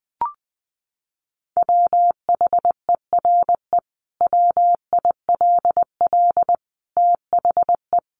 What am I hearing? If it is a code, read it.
Morse code, heard as a keyed tone, about 20 wpm, 700 Hz. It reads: WHERE WILL THE
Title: Sentences from Top 1500 Words 20wpm